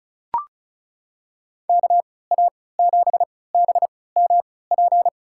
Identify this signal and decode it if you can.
Morse code, heard as a keyed tone, about 36 wpm, spaced as for 20 wpm, 700 Hz. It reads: KA7BMP
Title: USA Call Signs - 6 Characters - 3x Character Spacing - 35wpm